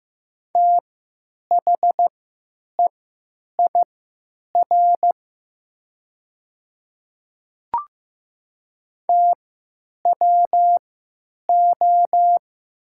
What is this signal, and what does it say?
Morse code, heard as a keyed tone, about 15 wpm, spaced as for 9 wpm, 700 Hz. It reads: THEIR TWO
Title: 3x Character Spacing - Top 100 Words - 15wpm